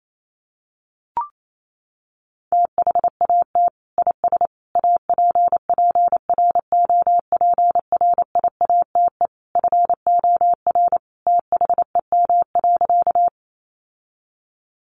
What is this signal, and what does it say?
Morse code, heard as a keyed tone, about 28 wpm, 700 Hz. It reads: THAT IS APPROPRIATE FOR THEM.
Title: Sentences from Top 900 Words 28wpm